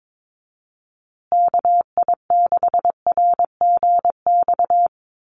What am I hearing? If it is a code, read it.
Morse code, heard as a keyed tone, about 22 wpm, 700 Hz. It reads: KI6RGX
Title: USA Call Signs Repeated 2x 22wpm